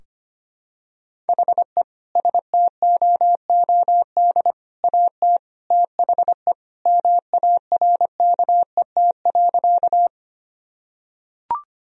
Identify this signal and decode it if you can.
Morse code, heard as a keyed tone, about 25 wpm, 700 Hz. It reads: HE STOOD AT THE MARKET.
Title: Sentences from Top 1000 Words - Review Part 3 - 25wpm